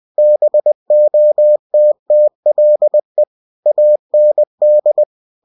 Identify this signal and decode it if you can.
Morse code, heard as a keyed tone, about 20 wpm, 600 Hz. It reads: BOTTLE AND
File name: cw-books / winnie-the-pooh / winnie-20wpm-page-0092